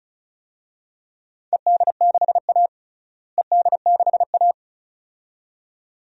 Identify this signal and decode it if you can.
Morse code, heard as a keyed tone, about 34 wpm, 700 Hz. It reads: ED6A ED6A
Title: DXCC-100 Call Signs - 4 Characters - Repeated 2x - 35wpm